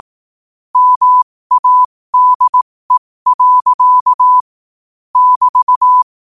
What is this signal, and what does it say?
Morse code, heard as a keyed tone, about 18 wpm, spaced as for 15 wpm, 1000 Hz. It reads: MADE. <BT>
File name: qotdfiles / 15wpm